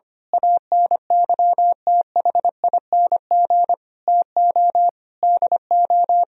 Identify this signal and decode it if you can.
Morse code, heard as a keyed tone, about 24 wpm, 700 Hz. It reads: ANYTHING TO DO
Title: Speed-Racing - Sets of 5 Words 25wpm